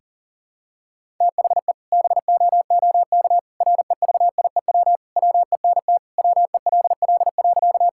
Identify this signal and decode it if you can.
Morse code, heard as a keyed tone, about 40 wpm, 700 Hz. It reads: THE BOOK REVIEW WENT WELL.
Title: Sentences from Top 1000 Words - Review Part 6 - 40wpm